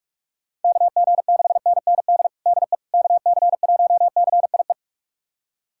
Morse code, heard as a keyed tone, about 41 wpm, 700 Hz. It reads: KG6NND DE KC1CIE
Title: USA Callsign DE Callsign - 6 Characters - 45wpm